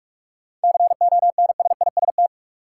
Morse code, heard as a keyed tone, about 46 wpm, 700 Hz. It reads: CONSIST
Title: Sentences from Top 1400 Words - New Word Drill - 45wpm